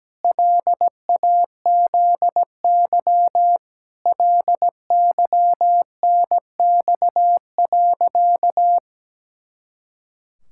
Morse code, heard as a keyed tone, about 17 wpm, 700 Hz. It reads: LAZY LYNX.
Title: Pangram Challenge v2 - 17wpm